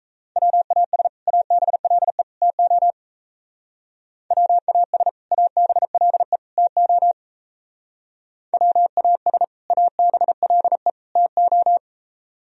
Morse code, heard as a keyed tone, about 34 wpm, 700 Hz. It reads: WAS ABLE TO WAS ABLE TO WAS ABLE TO
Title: Speed-Racing - Sets of 3 Words 28wpm